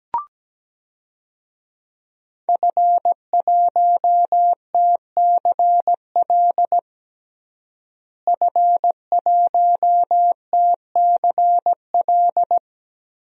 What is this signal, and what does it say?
Morse code, heard as a keyed tone, about 17 wpm, 700 Hz. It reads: F1TCL F1TCL
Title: DXCC-100 Call Signs - 5 Characters - Repeated 3x - 17wpm